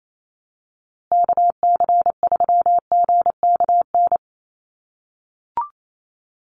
Morse code, heard as a keyed tone, about 28 wpm, 700 Hz. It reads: KC3GKN
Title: USA Call Signs Repeated 2x 28wpm